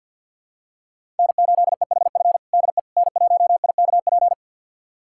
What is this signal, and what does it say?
Morse code, heard as a keyed tone, about 49 wpm, 700 Hz. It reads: N8EHF DE N1IKP
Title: USA Callsign DE Callsign - 5 Characters - 50wpm